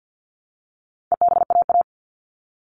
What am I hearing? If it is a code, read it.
Morse code, heard as a keyed tone, about 50 wpm, 700 Hz. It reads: EBAA
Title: All Cut Numbers - 4-Digit 50wpm